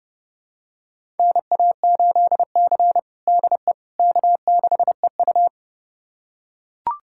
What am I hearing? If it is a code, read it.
Morse code, heard as a keyed tone, about 31 wpm, 700 Hz. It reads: NA8C DE K6EU
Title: USA Callsign DE Callsign - 4 Characters - 30wpm